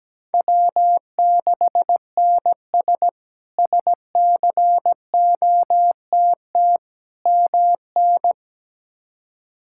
Morse code, heard as a keyed tone, about 17 wpm, 700 Hz. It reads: W6NS SCOTT MN
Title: K1USN SST QSO Practice - 17wpm